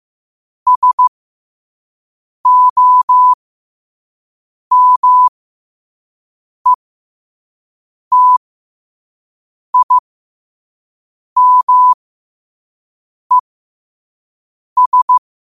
Morse code, heard as a keyed tone, about 15 wpm, spaced as for 5 wpm, 1000 Hz. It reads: SOMETIMES
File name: qotdfiles / 5wpm